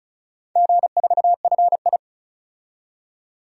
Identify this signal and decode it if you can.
Morse code, heard as a keyed tone, about 36 wpm, 700 Hz. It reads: G4FI
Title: DXCC-100 Call Signs - 4 Characters - 35wpm